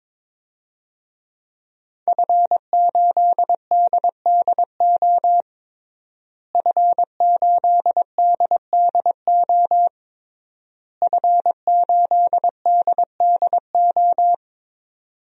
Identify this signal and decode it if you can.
Morse code, heard as a keyed tone, about 22 wpm, 700 Hz. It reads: F8DDO F8DDO F8DDO
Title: DXCC-100 Call Signs - 5 Characters - Repeated 3x - 22wpm